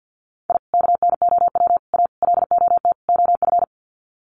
Morse code, heard as a keyed tone, about 50 wpm, 700 Hz. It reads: I KNOW A LOT OF
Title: Sets of 5 Words Encore 50wpm